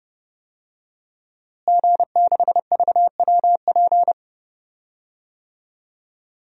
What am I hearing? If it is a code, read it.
Morse code, heard as a keyed tone, about 30 wpm, 700 Hz. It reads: G6VWP
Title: DXCC-100 Call Signs - 5 Characters - Repeated 2x - 30wpm